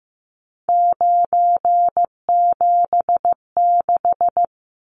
Morse code, heard as a keyed tone, about 15 wpm, 700 Hz. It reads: 976
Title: Numbers - 3-Digit 15wpm